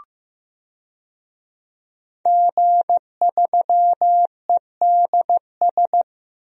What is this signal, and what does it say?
Morse code, heard as a keyed tone, about 15 wpm, 700 Hz. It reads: G3EDS
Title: DXCC-100 Call Signs - 5 Characters - 15wpm